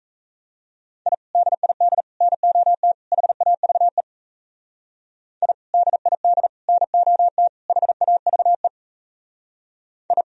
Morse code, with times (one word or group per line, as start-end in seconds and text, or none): 1.1-1.1 I
1.4-2.0 DID
2.2-2.9 NOT
3.1-4.0 HAVE
5.4-5.5 I
5.7-6.5 DID
6.7-7.5 NOT
7.7-8.7 HAVE
10.1-10.2 I